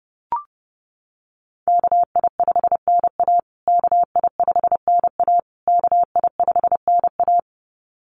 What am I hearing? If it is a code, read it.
Morse code, heard as a keyed tone, about 30 wpm, 700 Hz. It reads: KI5NA KI5NA KI5NA
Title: USA Call Signs Repeated 3x 30wpm